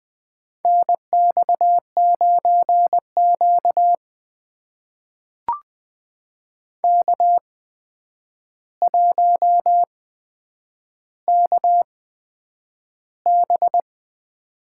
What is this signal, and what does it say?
Morse code, heard as a keyed tone, about 20 wpm, spaced as for 9 wpm, 700 Hz. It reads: NX9Q K1KB
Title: USA Call Signs - 4 Characters - 8x Character Spacing - 20wpm